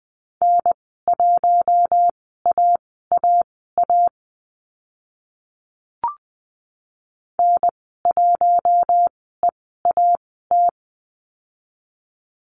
Morse code, heard as a keyed tone, about 20 wpm, spaced as for 15 wpm, 700 Hz. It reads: N1AAA N1EAT
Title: 042 Lesson 8g - Number 1 - Call Signs 20wpm